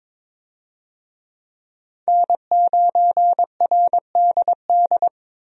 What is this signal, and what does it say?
Morse code, heard as a keyed tone, about 22 wpm, 700 Hz. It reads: N9RDD
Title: USA Call Signs - 5 Characters - 22wpm